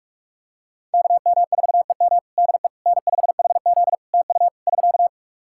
Morse code, heard as a keyed tone, about 46 wpm, 700 Hz. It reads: KM4EM DE N5HZ TU <SK>
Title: Practice QSO - Exchange 4 - Ending - 45wpm